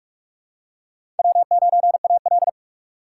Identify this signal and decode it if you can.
Morse code, heard as a keyed tone, about 45 wpm, 700 Hz. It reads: W9AL
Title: USA Call Signs - 4 Characters - 6x Character Spacing - 45wpm